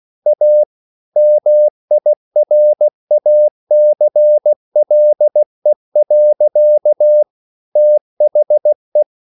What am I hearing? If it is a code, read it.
Morse code, heard as a keyed tone, about 16 wpm, 600 Hz. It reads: A MIRACLE. THE
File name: CW / Practice / EnglishQuotations0416